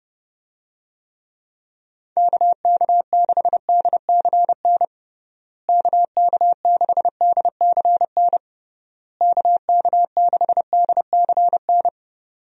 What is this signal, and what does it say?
Morse code, heard as a keyed tone, about 32 wpm, 700 Hz. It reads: KK6DCN KK6DCN KK6DCN
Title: USA Call Signs - 6 Characters - Repeated 3x - 30wpm